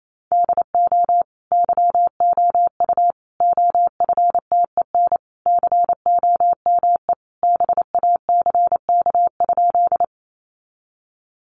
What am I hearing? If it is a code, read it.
Morse code, heard as a keyed tone, about 28 wpm, 700 Hz. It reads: DO YOU OFTEN COME BACK?
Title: Sentences from Top 300 Words 28wpm